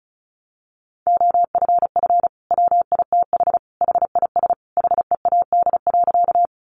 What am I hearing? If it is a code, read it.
Morse code, heard as a keyed tone, about 34 wpm, 700 Hz. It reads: OFF WITH HIS HEAD.
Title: Sentences from Top 300 Words 35wpm